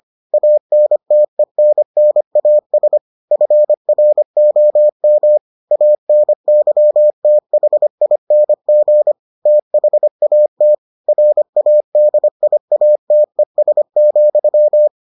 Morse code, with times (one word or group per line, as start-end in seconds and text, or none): 0.3-3.0 ANTENNAS
3.3-5.4 FROM
5.7-9.1 ANYTHING
9.5-10.7 THAT
11.1-14.9 RADIATES,